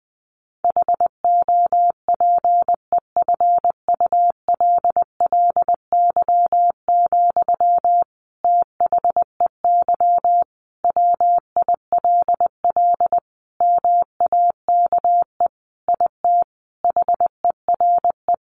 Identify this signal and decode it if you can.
Morse code, heard as a keyed tone, about 20 wpm, 700 Hz. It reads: HOPEFULLY, THEY WILL MAKE IT HERE
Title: Sentences from Top 1500 Words 20wpm